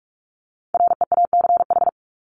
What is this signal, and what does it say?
Morse code, heard as a keyed tone, about 44 wpm, 700 Hz. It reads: REACH
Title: Top 500 Words Repeated 45wpm